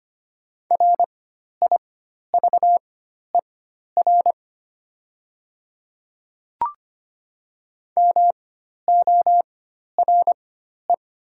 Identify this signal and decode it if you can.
Morse code, heard as a keyed tone, about 25 wpm, spaced as for 12 wpm, 700 Hz. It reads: RIVER MORE
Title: 4x Character Spacing - Top 300 Words - 25wpm